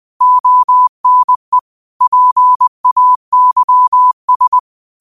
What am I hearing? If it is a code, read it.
Morse code, heard as a keyed tone, about 20 wpm, 1000 Hz. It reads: ONE PAYS
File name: qotdfiles / 20wpm